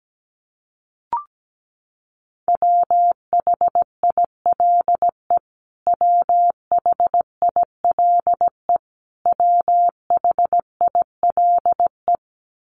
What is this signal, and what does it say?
Morse code, heard as a keyed tone, about 17 wpm, 700 Hz. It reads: WHILE WHILE WHILE
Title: Top 300 Words Repeated 17wpm